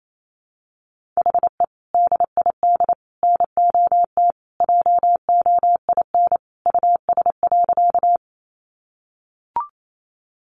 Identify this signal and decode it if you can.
Morse code, heard as a keyed tone, about 28 wpm, 700 Hz. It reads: HE DID NOT JOIN US.
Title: Sentences from Top 1400 Words 28wpm